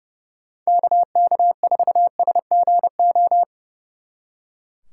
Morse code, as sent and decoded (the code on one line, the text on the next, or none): -.- -.- ....- ... --. ---
KK4SGO